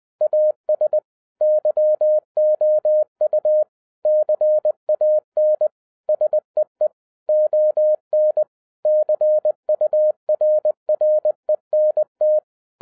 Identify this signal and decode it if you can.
Morse code, heard as a keyed tone, about 20 wpm, 610 Hz. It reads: AS YOU CAN SEE ON CURRENT